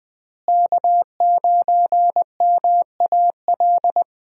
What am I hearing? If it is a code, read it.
Morse code, heard as a keyed tone, about 20 wpm, 700 Hz. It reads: K9MAL
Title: USA Call Signs - 5 Characters - Repeated 3x - 20wpm